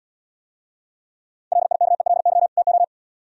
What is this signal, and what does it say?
Morse code, heard as a keyed tone, about 75 wpm, 700 Hz. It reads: LIKELY TO